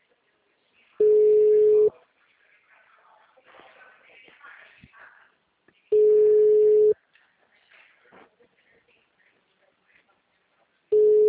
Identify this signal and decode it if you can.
Ringback tone (Europe, 425 Hz).
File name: out- -2021- - - .28030